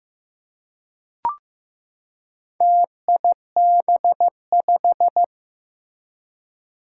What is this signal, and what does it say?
Morse code, heard as a keyed tone, about 15 wpm, 700 Hz. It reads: TIB5
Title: Four Challenging Letter-Number Combinations 15wpm